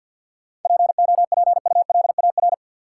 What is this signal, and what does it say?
Morse code, heard as a keyed tone, about 51 wpm, 700 Hz. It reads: POPULAR
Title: Sentences from Top 1200 Words - New Word Drill Alt - 50wpm